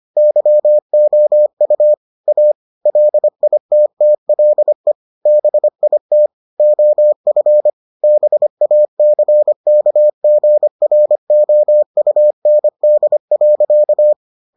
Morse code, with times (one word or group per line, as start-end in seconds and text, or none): 0.2-1.9 YOU
2.3-2.5 A
2.9-4.9 LITTLE
5.3-6.3 BIT
6.6-7.7 OF
8.0-14.1 BACKGROUND.